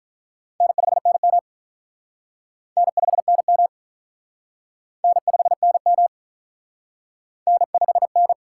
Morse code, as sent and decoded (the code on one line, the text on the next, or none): -. ..... -. -- / -. ..... -. -- / -. ..... -. -- / -. ..... -.
N5NM N5NM N5NM N5N